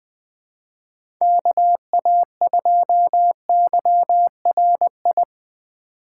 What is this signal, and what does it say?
Morse code, heard as a keyed tone, about 20 wpm, 700 Hz. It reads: KA2YRI
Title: USA Call Signs - 2x Character Spacing - 20wpm